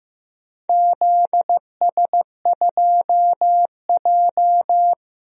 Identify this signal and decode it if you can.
Morse code, heard as a keyed tone, about 15 wpm, 700 Hz. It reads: ZS2J
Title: DXCC-100 Call Signs - 4 Characters - Repeated 3x - 15wpm